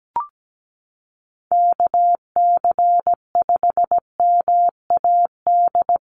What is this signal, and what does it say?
Morse code, heard as a keyed tone, about 17 wpm, 700 Hz. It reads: KC5MAD
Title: USA Call Signs Repeated 3x 17wpm